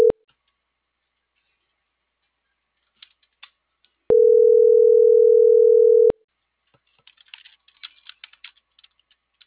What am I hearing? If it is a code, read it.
Ringback tone (North America), 440 + 480 Hz.